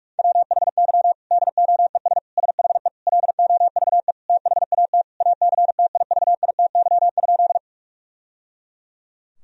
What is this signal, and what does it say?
Morse code, heard as a keyed tone, about 41 wpm, 700 Hz. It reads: WHY DOES SHE LOVE THAT ACTIVITY?
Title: Sentences from Top 1000 Words - Review Part 4 - 45wpm